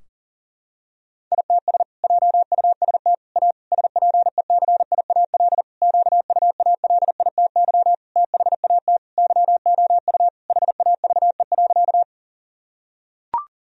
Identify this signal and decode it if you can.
Morse code, heard as a keyed tone, about 40 wpm, 700 Hz. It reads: ITS JUST A SPECIAL QUALITY THAT YOU HAVE.
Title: Sentences from Top 1000 Words - Review Part 4 - 40wpm